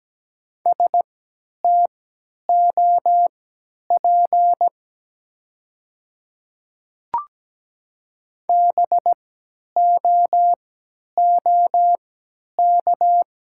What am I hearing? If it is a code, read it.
Morse code, heard as a keyed tone, about 17 wpm, spaced as for 10 wpm, 700 Hz. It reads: STOP BOOK
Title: 3x Character Spacing - Top 300 Words - 17wpm